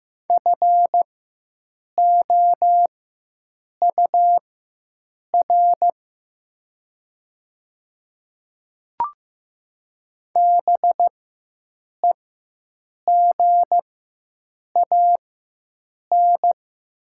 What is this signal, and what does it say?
Morse code, heard as a keyed tone, about 15 wpm, spaced as for 7 wpm, 700 Hz. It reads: FOUR BEGAN
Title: 4x Character Spacing - Top 500 Words - 15wpm